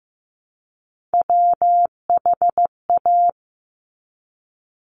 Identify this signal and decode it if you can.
Morse code, heard as a keyed tone, about 15 wpm, 700 Hz. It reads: WHA
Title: Top Three Letter Combinations in Words 15wpm